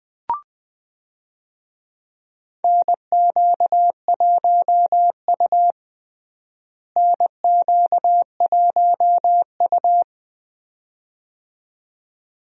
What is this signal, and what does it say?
Morse code, heard as a keyed tone, about 20 wpm, 700 Hz. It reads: NQ1U NQ1U
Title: USA Call Signs - 4 Characters - Repeated 2x - 20wpm